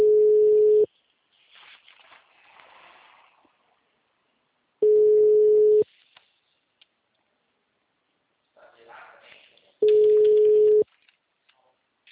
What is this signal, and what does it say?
Ringback tone (Europe, 425 Hz), with speech.